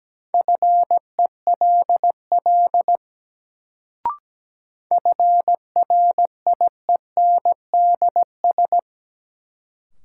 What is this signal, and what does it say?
Morse code, heard as a keyed tone, about 17 wpm, 700 Hz. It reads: FELL FRIENDS
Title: Sentences from Top 1600 Words - New Word Drill Alt - 17wpm